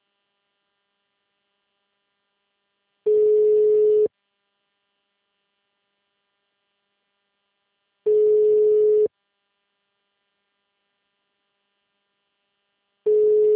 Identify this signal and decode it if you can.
Ringback tone (Europe, 425 Hz).